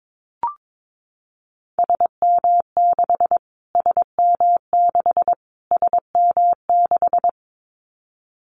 Morse code, heard as a keyed tone, about 22 wpm, 700 Hz. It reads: SM6 SM6 SM6
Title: Callsign Prefixes for Top 25 DXCC - Repeated 3x - 22wpm